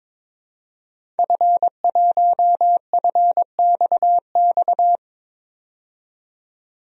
Morse code, heard as a keyed tone, about 22 wpm, 700 Hz. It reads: F1FXX
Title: DXCC-100 Call Signs - 5 Characters - 22wpm